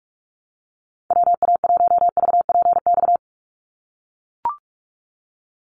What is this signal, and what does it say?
Morse code, heard as a keyed tone, about 44 wpm, 700 Hz. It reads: WA1VPX
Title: USA Call Signs 45wpm